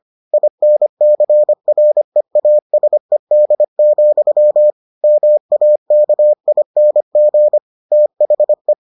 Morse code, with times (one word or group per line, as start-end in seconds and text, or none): 0.3-4.7 INCREASED,
5.0-7.6 MAKING
7.9-8.7 THE